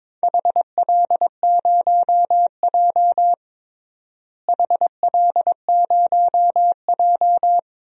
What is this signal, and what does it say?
Morse code, heard as a keyed tone, about 22 wpm, 700 Hz. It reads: HL0J HL0J
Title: DXCC-100 Call Signs - 4 Characters - Repeated 3x - 22wpm